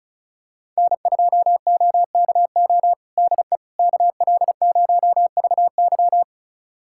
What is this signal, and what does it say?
Morse code, heard as a keyed tone, about 34 wpm, 700 Hz. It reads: N2OKO DE KL0VY
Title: USA Callsign DE Callsign 35wpm